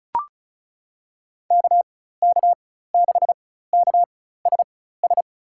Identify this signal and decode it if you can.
Morse code, heard as a keyed tone, about 34 wpm, spaced as for 16 wpm, 700 Hz. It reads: KK6KSS
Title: USA Call Signs - 6 Characters - 4x Character Spacing - 35wpm